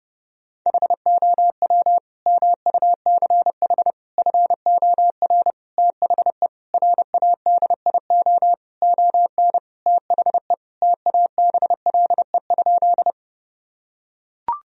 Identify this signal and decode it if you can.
Morse code, heard as a keyed tone, about 30 wpm, 700 Hz. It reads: HOW MUCH FOR THE RADIO ON THE TABLE?
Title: Sentences from Top 1000 Words - Review Part 3 - 30wpm